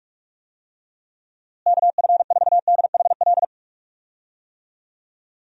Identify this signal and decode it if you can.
Morse code, heard as a keyed tone, about 46 wpm, 700 Hz. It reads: KF4DHL
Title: USA Call Signs - 6 Characters - Repeated 2x - 45wpm